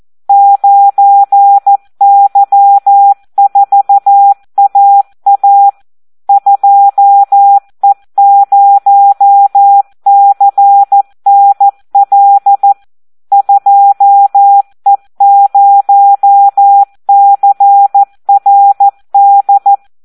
Morse code, heard as a keyed tone, about 14 wpm, 800 Hz. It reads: 9Y4AA 2E0CNL 2E0CRD
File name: Audio practica CW / SD 1 14